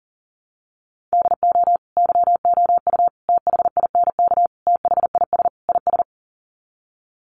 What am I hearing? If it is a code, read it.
Morse code, heard as a keyed tone, about 40 wpm, 700 Hz. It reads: DO YOU THINK THIS IS
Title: Sets of 5 Words Encore2 40wpm